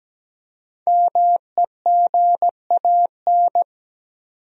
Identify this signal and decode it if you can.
Morse code, heard as a keyed tone, about 17 wpm, 700 Hz. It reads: MEGAN
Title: USA Common Names - Max 5 Letters - 17wpm